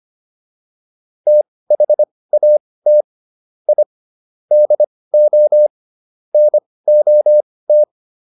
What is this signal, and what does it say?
Morse code, heard as a keyed tone, about 25 wpm, spaced as for 18 wpm, 600 Hz. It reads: THAT I DO NOT